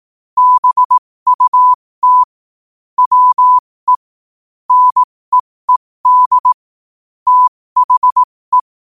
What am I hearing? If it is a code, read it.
Morse code, heard as a keyed tone, about 18 wpm, spaced as for 15 wpm, 1000 Hz. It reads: BUT WE NEED THE
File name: qotdfiles / 15wpm